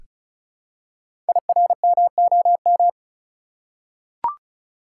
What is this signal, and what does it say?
Morse code, heard as a keyed tone, about 36 wpm, 690 Hz. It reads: IRMOM
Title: Code Groups - 5 Letters - 35wpm